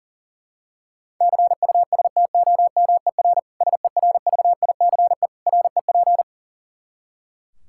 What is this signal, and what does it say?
Morse code, heard as a keyed tone, about 40 wpm, 700 Hz. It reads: CUSTOMER SERVICE REP
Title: Practice QSO - Exchange 3 - More Info - 40wpm